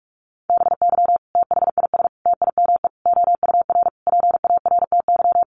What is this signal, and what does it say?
Morse code, heard as a keyed tone, about 44 wpm, 700 Hz. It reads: BY THIS TIME OUR PARTY
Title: Sentences from Top 1400 Words 45wpm